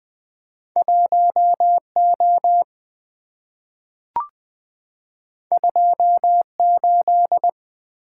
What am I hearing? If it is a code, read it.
Morse code, heard as a keyed tone, about 20 wpm, 700 Hz. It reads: 1O 28
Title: Two Letter-Number Combinations - 20wpm